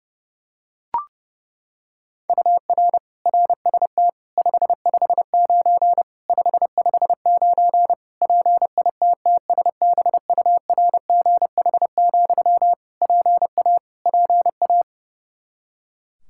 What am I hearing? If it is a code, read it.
Morse code, heard as a keyed tone, about 30 wpm, 700 Hz. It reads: UR RST 559 559 PITTSBURGH, PA PA
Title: Practice QSO - Exchange 1 - Beginning - 30wpm